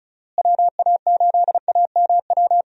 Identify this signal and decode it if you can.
Morse code, heard as a keyed tone, about 36 wpm, 700 Hz. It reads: WA8AMW
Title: USA Call Signs - 6 Characters - Repeated 3x - 35wpm